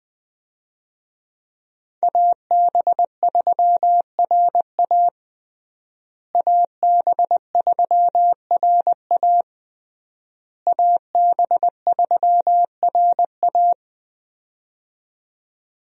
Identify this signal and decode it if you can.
Morse code, heard as a keyed tone, about 20 wpm, 700 Hz. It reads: AB3RA AB3RA AB3RA
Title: USA Call Signs - 5 Characters - Repeated 3x - 20wpm